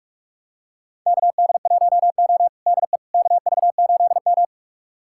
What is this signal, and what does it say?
Morse code, heard as a keyed tone, about 47 wpm, 700 Hz. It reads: KD1O DE KV8M
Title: USA Callsign DE Callsign - 4 Characters - 45wpm